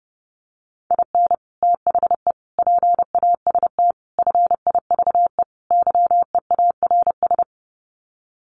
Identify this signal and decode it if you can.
Morse code, heard as a keyed tone, about 30 wpm, 700 Hz. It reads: IN THE PAST FIVE YEARS
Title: Sets of 5 Words Encore 30wpm